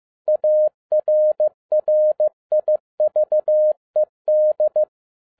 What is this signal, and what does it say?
Morse code, heard as a keyed tone, about 15 wpm, 610 Hz. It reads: ARRIVED
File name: 2011-07-mp3 / n47 JOHN K3TN 15wpm